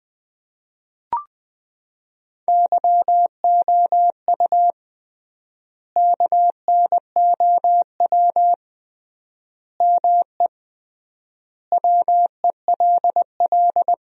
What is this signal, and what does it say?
Morse code, heard as a keyed tone, about 20 wpm, 700 Hz. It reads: YOU KNOW ME WELL
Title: 3x Word-Spacing - Sentences from Top 400 Words - 20wpm